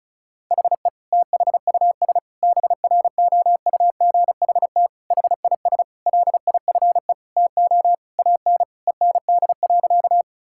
Morse code, heard as a keyed tone, about 36 wpm, 700 Hz. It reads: HE THUS BROUGHT HIS LIFE TO AN END.
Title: Sentences from Top 1000 Words - Review Part 5 - 35wpm